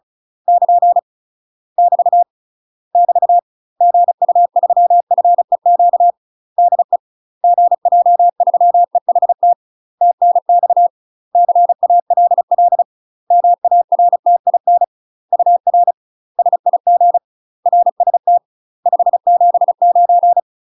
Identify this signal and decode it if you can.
Morse code, heard as a keyed tone, about 34 wpm, 700 Hz. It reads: <KN> <BT> <BT> GU3FEQ DE GJ3EHT TNX CALL MARTIN UR SIG RST 579
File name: UK MASTER QSOs TRAINING / 35WPM / UK MASTER QSOs 0001